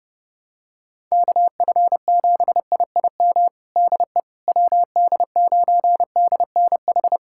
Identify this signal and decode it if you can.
Morse code, heard as a keyed tone, about 30 wpm, 700 Hz. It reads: KF7IIM DE WD9DNH
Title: USA Callsign DE Callsign - 6 Characters - 30wpm